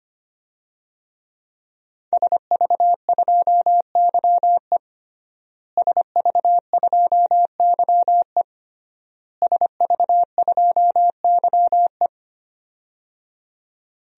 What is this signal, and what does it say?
Morse code, heard as a keyed tone, about 25 wpm, 700 Hz. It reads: SV2YE SV2YE SV2YE
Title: DXCC-100 Call Signs - 5 Characters - Repeated 3x - 25wpm